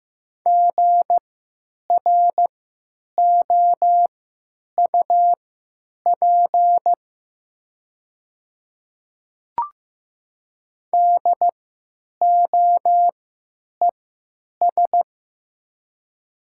Morse code, heard as a keyed tone, about 15 wpm, spaced as for 9 wpm, 700 Hz. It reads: GROUP DOES
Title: 3x Character Spacing - Top 300 Words - 15wpm